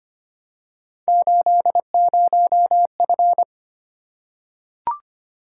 Morse code, heard as a keyed tone, about 25 wpm, 700 Hz. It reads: 80F
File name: Three Letter-Number Combinations - 25wpm